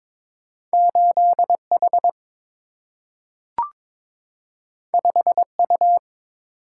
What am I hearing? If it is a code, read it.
Morse code, heard as a keyed tone, about 21 wpm, 700 Hz. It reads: 8H 5U
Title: Single-Letter Single-Number 22wpm